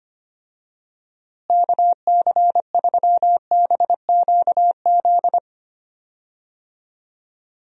Morse code, heard as a keyed tone, about 25 wpm, 700 Hz. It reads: KC3BQZ
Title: USA Call Signs - 6 Characters - Repeated 2x - 25wpm